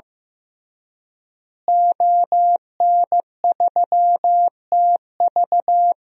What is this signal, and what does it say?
Morse code, heard as a keyed tone, about 16 wpm, 700 Hz. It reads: ON3TV
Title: DXCC-100 Call Signs - 5 Characters - Repeated 3x - 15wpm